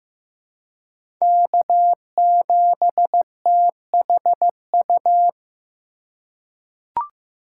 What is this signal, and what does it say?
Morse code, heard as a keyed tone, about 15 wpm, 700 Hz. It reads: K7THU
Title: USA Call Signs - 5 Characters - 3x Character Spacing - 15wpm